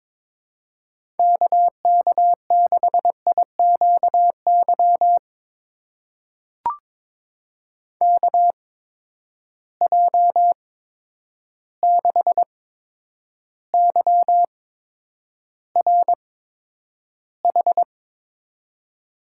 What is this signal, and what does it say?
Morse code, heard as a keyed tone, about 21 wpm, spaced as for 9 wpm, 700 Hz. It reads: KK6IQY KJ6YRH
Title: USA Call Signs - 6 Characters - 8x Character Spacing - 22wpm